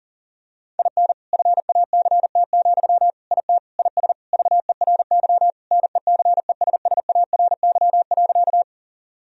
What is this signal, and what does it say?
Morse code, heard as a keyed tone, about 40 wpm, 700 Hz. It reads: IN FACT, IT IS VERY NECESSARY.
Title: Sentences from Top 1000 Words - Review Part 3 - 40wpm